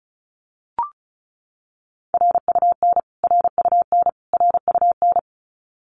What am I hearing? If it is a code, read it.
Morse code, heard as a keyed tone, about 34 wpm, 700 Hz. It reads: RUN RUN RUN
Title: Top 300 Words Repeated 35wpm